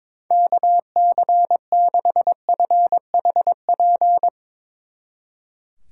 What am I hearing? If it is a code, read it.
Morse code, heard as a keyed tone, about 22 wpm, 700 Hz. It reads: KC6FHP
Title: Speed Racing - USA 2x3 Call Signs 22wpm